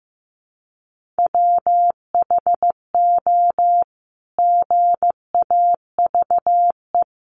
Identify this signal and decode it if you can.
Morse code, heard as a keyed tone, about 15 wpm, 700 Hz. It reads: WHO GAVE
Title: Sentences from Top 900 Words 15wpm